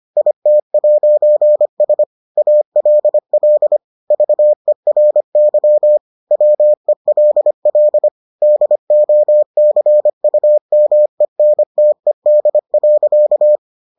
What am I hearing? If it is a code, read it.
Morse code, heard as a keyed tone, about 25 wpm, 600 Hz. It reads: IT'S ALL VERY WELL DOCUMENTED.